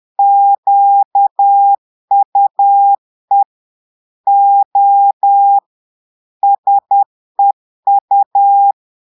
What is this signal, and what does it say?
Morse code, heard as a keyed tone, about 10 wpm, 800 Hz. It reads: QUE O SEU
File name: portugais texte en clair 2 vitesse 10 mots minutes